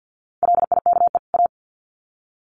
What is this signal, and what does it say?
Morse code, heard as a keyed tone, about 50 wpm, 700 Hz. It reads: LIKE A
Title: Sets of 2 Words 50wpm